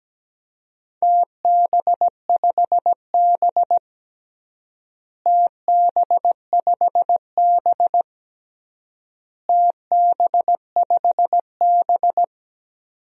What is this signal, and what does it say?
Morse code, heard as a keyed tone, about 17 wpm, 700 Hz. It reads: TB5B TB5B TB5B
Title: DXCC-100 Call Signs - 4 Characters - Repeated 3x - 17wpm